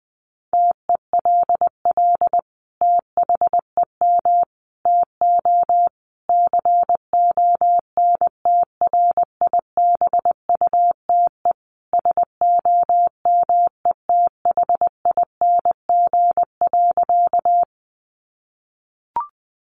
Morse code, heard as a keyed tone, about 20 wpm, 700 Hz. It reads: TELL THEM TO CONTRIBUTE SOMETHING.
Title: Sentences from Top 1400 Words 20wpm